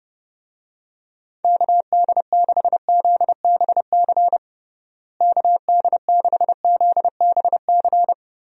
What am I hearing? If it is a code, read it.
Morse code, heard as a keyed tone, about 32 wpm, 700 Hz. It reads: KD6ZBC KD6ZBC
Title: USA Call Signs - 6 Characters - Repeated 3x - 30wpm